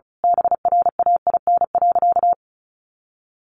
Morse code, heard as a keyed tone, about 34 wpm, 700 Hz. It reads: BRAIN.
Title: A Sound Of Thunder 35wpm